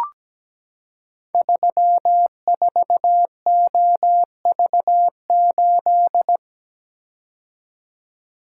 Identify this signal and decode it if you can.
Morse code, heard as a keyed tone, about 17 wpm, 700 Hz. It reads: 34OV8
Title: Code Groups - 5 Letter:Numbers - 17wpm